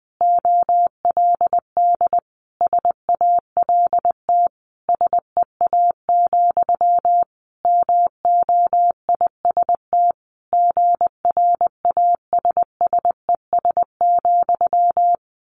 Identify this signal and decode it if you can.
Morse code, heard as a keyed tone, about 20 wpm, 700 Hz. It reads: OLD SALT SEA, MOIST GRASSES,
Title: A Sound Of Thunder 20wpm